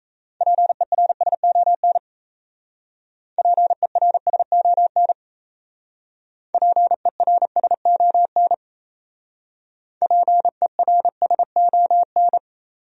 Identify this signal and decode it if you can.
Morse code, heard as a keyed tone, about 34 wpm, 700 Hz. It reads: PERSON PERSON PERSON PERSON
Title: Speed-Racing Top 500 Words 28wpm